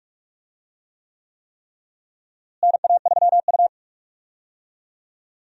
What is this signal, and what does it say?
Morse code, heard as a keyed tone, about 44 wpm, 700 Hz. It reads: NA3U
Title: USA Call Signs - 4 Characters - Repeated 3x - 45wpm